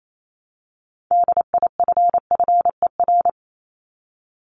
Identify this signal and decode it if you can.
Morse code, heard as a keyed tone, about 28 wpm, 700 Hz. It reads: DIFFER
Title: Top 500 Words 28wpm